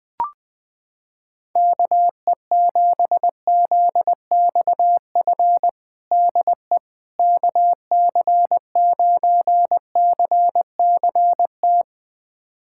Morse code, heard as a keyed tone, about 19 wpm, 700 Hz. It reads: KE7ZXF DE KC9CCT
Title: USA Callsign DE Callsign - 6 Characters - 20wpm